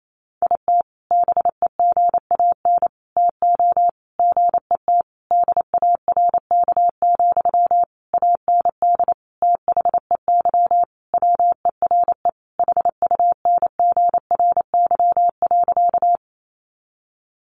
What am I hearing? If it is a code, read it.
Morse code, heard as a keyed tone, about 28 wpm, 700 Hz. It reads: IT BEGAN TO GET DARK, AND THEY WERE HUNGRY.
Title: Sentences from Top 1400 Words 28wpm